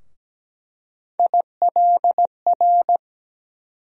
Morse code, heard as a keyed tone, about 17 wpm, 700 Hz. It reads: ILR
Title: USA Call Sign Postfixes - 3 Letters 17wpm